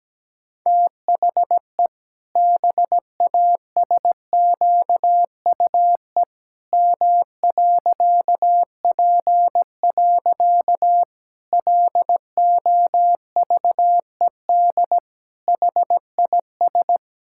Morse code, heard as a keyed tone, about 17 wpm, 700 Hz. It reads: THE BASQUE M.P. LOVED HIS
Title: Pangram Challenge v2 - 17wpm